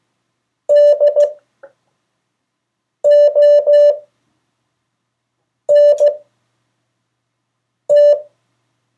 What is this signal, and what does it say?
Morse code, heard as a keyed tone, about 15 wpm, spaced as for 4 wpm, 580 Hz. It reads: DONT
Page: CW Class Review – Delaware Valley Radio Association